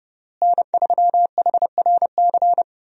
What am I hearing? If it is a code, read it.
Morse code, heard as a keyed tone, about 30 wpm, 700 Hz. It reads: N3HRC
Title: USA Call Signs - 5 Characters - Repeated 3x - 30wpm